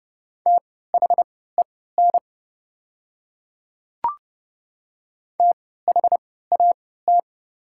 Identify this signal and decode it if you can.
Morse code, heard as a keyed tone, about 30 wpm, spaced as for 17 wpm, 700 Hz. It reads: THEN THAT
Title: 3x Character Spacing - Top 100 Words - 30wpm